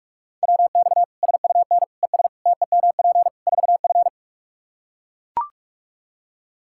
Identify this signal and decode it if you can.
Morse code, heard as a keyed tone, about 44 wpm, 700 Hz. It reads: WX SUN ES TEMP 4F
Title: Practice QSO - Exchange 2 - Info - 45wpm